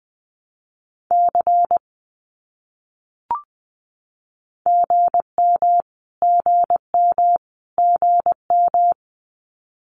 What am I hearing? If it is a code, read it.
Morse code, heard as a keyed tone, about 20 wpm, 700 Hz. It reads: C GM GM GM
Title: Abbreviations Repeated 3x - 20wpm